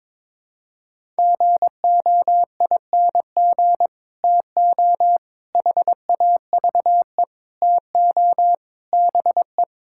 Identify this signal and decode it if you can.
Morse code, heard as a keyed tone, about 22 wpm, 700 Hz. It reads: GOING TO HAVE TO BE
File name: practice / 22wpm / Build A Phrase - 5-Word Phrases - 22wpm